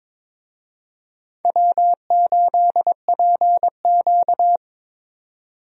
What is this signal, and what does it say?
Morse code, heard as a keyed tone, about 21 wpm, 700 Hz. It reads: W8PQ
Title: USA Call Signs - 4 Characters - Repeated 2x - 22wpm